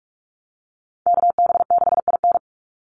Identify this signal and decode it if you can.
Morse code, heard as a keyed tone, about 44 wpm, 700 Hz. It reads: KB6IN
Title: USA Call Signs 45wpm